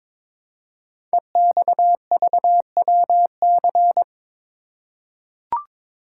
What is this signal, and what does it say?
Morse code, heard as a keyed tone, about 23 wpm, 700 Hz. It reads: EXVWC
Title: Code Groups - 5 Letters - 22wpm